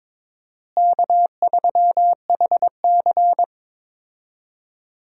Morse code, heard as a keyed tone, about 22 wpm, 700 Hz. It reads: K3HC
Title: USA Call Signs - 4 Characters - Repeated 3x - 22wpm